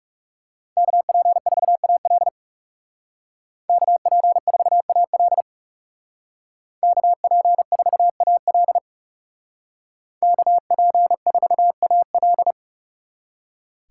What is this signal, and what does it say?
Morse code, heard as a keyed tone, about 39 wpm, 700 Hz. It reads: KP4AL KP4AL KP4AL KP4AL